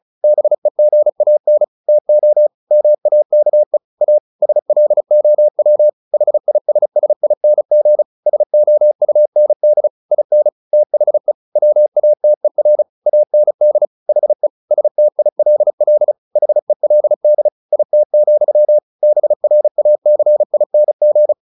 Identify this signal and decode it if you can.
Morse code, heard as a keyed tone, about 34 wpm, 600 Hz. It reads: BEGAN TO MAKE A SLOW HISSING SOUND IN THE WATER AND HE STILL HELD IT, BRACING
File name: the old man and the sea / the old man and the sea 1 35wpm